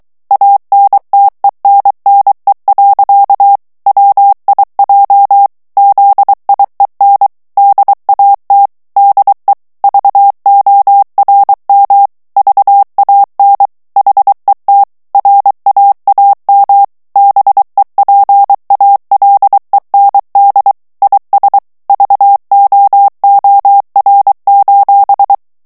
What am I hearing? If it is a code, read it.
Morse code, heard as a keyed tone, about 23 wpm, 800 Hz. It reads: ANTENNE. WIJ ZIEN DAT DE VORM VAN HET RAAM BEPALEND IS VOOR:
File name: HOLLANDAIS texte clair 1 vitesse 22 mots minute